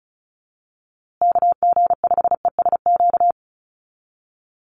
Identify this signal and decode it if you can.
Morse code, heard as a keyed tone, about 34 wpm, 700 Hz. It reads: KG5ESQ
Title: USA Call Signs Repeated 3x 35wpm